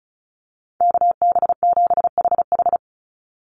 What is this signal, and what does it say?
Morse code, heard as a keyed tone, about 34 wpm, 700 Hz. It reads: KB7HH
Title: NA SOTA Chaser Callsigns 35wpm